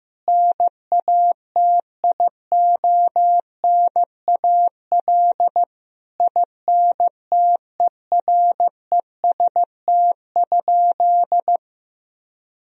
Morse code, heard as a keyed tone, about 15 wpm, 700 Hz. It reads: NATIONAL INTEREST?
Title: Sentences from Top 1000 Words - Review Part 4 - 15wpm